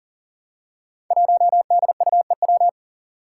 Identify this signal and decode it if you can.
Morse code, heard as a keyed tone, about 38 wpm, 700 Hz. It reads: 1DUEW
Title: Five Challenging Letter-Number Combinations 40wpm